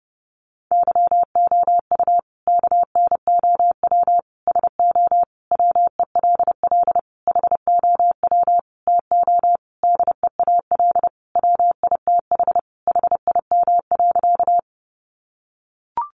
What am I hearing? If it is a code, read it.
Morse code, heard as a keyed tone, about 30 wpm, 700 Hz. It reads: YOU KNOW SO WELL HOW TO DEAL WITH HIM.
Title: Sentences from Top 800 Words 30wpm